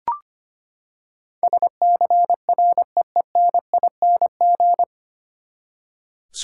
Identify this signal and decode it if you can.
Morse code, heard as a keyed tone, about 25 wpm, 700 Hz. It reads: SCREENING
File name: Nine Letter Words - 25wpm